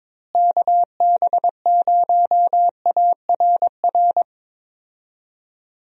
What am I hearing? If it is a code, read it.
Morse code, heard as a keyed tone, about 22 wpm, 700 Hz. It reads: KB0ARR
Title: USA Call Signs - 6 Characters - Repeated 3x - 22wpm